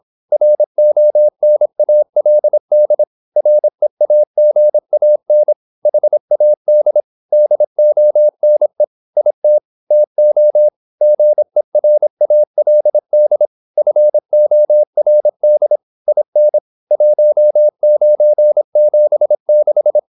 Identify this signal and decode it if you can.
Morse code, heard as a keyed tone, about 26 wpm, 600 Hz. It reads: RONALD REAGAN HAD DONE IT TO GERALD FORD IN 1976
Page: Never 26wpm | CW med Gnister